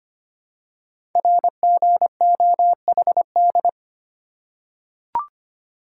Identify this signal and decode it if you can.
Morse code, heard as a keyed tone, about 25 wpm, 700 Hz. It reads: RGOHD
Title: Code Groups - 5 Letters - 25wpm